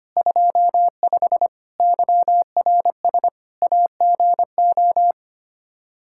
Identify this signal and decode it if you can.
Morse code, heard as a keyed tone, about 25 wpm, 700 Hz. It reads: 25 YRS AGO
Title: Practice QSO - Exchange 3 - More Info - 25wpm